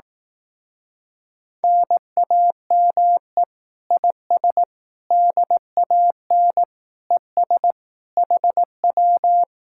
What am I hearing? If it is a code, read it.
Morse code, heard as a keyed tone, about 18 wpm, 700 Hz. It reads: NAME IS DAN ES HW
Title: Speed-Racing Common QSO Phrases Encore - 15wpm